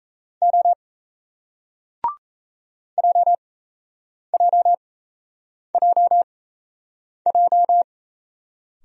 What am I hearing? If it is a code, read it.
Morse code, heard as a keyed tone, about 38 wpm, spaced as for 8 wpm, 700 Hz. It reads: O JJJJ